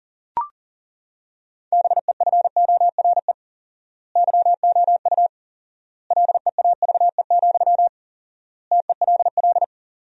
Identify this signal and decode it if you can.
Morse code, heard as a keyed tone, about 40 wpm, 700 Hz. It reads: BEFORE YOU LEAVE, TELL
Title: 4x Word-Spacing - Sentences from Top 400 Words - 40wpm